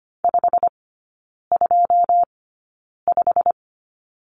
Morse code, heard as a keyed tone, about 25 wpm, spaced as for 9 wpm, 690 Hz. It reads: 525
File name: Lesson 15e - Number 5 - Three Characters 25wpm